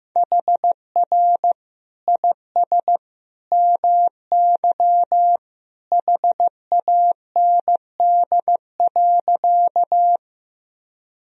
Morse code, heard as a keyed tone, about 15 wpm, 700 Hz. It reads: HR IS MY HAND.
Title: Sentences from Top 200 Words and Abbreviations 15wpm